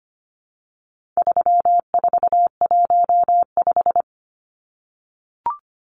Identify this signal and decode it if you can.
Morse code, heard as a keyed tone, about 25 wpm, 700 Hz. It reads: 3415
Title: Numbers - 4-Digit 25wpm